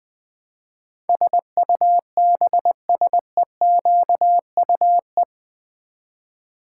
Morse code, heard as a keyed tone, about 20 wpm, 700 Hz. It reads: SUBSEQUE
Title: Build A Word - 4 Letters At A Time - 12-Letter Words - 20wpm